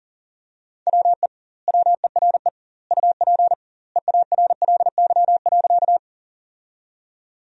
Morse code, heard as a keyed tone, about 40 wpm, 700 Hz. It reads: WE WERE UP EARLY.
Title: 2x Word-Spacing - Sentences from Top 400 Words - 40wpm